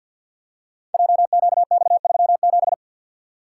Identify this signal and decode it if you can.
Morse code, heard as a keyed tone, about 50 wpm, 690 Hz. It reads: JQX37
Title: Code Groups - 5 Letter:Numbers - 50wpm